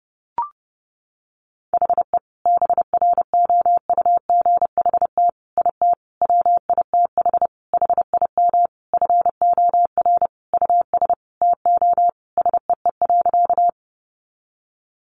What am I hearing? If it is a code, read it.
Morse code, heard as a keyed tone, about 28 wpm, 700 Hz. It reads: HE BROUGHT IT WITH HIM FOR US TO SEE.
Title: Sentences from Top 500 Words 30wpm